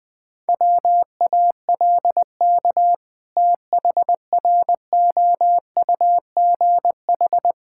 Morse code, heard as a keyed tone, about 20 wpm, 700 Hz. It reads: WALK THROUGH
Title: Sentences with 2-Word Phrases - 20wpm